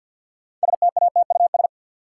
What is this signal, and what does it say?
Morse code, heard as a keyed tone, about 45 wpm, 700 Hz. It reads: STATUS
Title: Sentences from Top 1400 Words - New Word Drill - 50wpm